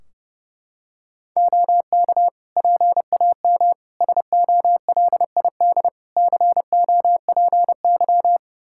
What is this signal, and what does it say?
Morse code, heard as a keyed tone, about 30 wpm, 700 Hz. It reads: OK PAM SOLID COPY
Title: Practice QSO - Exchange 3 - More Info - 30wpm